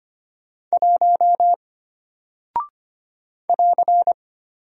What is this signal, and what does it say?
Morse code, heard as a keyed tone, about 25 wpm, 700 Hz. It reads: 1 <AR>
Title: Single Character (Letter:Number:Punctuation:Prosign) Alt 25wpm